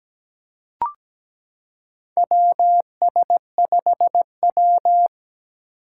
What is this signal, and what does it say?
Morse code, heard as a keyed tone, about 17 wpm, 700 Hz. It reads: WS5W
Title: Four Challenging Letter-Number Combinations 17wpm